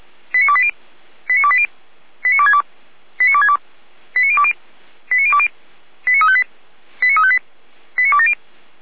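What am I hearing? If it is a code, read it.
Five-tone selective call (ZVEI), heard as quick runs of five tones, tones of about 70 ms. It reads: 89280 89280 89262 89262 89020 89020 89358 89358 89270
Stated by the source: ZVEI five-tone call received by SDR, fire brigade Orlamünde (feuerwehr-orlamuende) audio archive.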